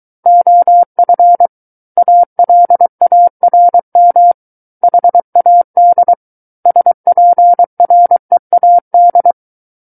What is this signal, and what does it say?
Morse code, heard as a keyed tone, about 23 wpm, 700 Hz. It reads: OF ALARM HAD SPREAD
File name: Book One / 14-Chapter 14 - 23 WPM